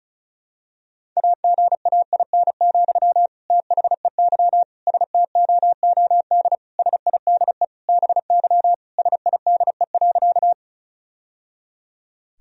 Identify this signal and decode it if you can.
Morse code, heard as a keyed tone, about 34 wpm, 700 Hz. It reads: AGAIN, THEY STOOD SIDE BY SIDE.
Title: Sentences from Top 1000 Words - Review Part 2 - 35wpm